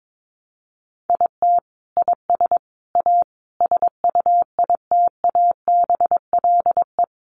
Morse code, heard as a keyed tone, about 22 wpm, 700 Hz. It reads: IT IS A SUITABLE
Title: Sentences from Top 1400 Words 22wpm